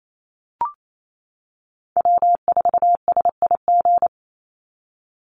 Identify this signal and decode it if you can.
Morse code, heard as a keyed tone, about 28 wpm, 700 Hz. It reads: W4SIG
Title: Call Signs - 28wpm